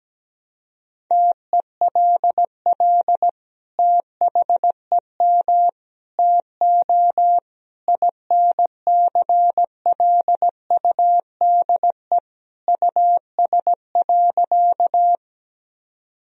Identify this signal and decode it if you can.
Morse code, heard as a keyed tone, about 17 wpm, 700 Hz. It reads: TELL THEM TO INCLUDE US.
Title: Sentences from Top 1000 Words - Review Part 3 - 17wpm